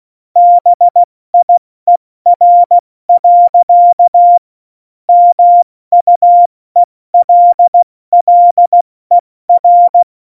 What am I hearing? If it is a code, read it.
Morse code, heard as a keyed tone, about 16 wpm, 700 Hz. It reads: BIER. MUELLER
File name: Morsekurs / Witze / Witz 17. November 2020, 70 F80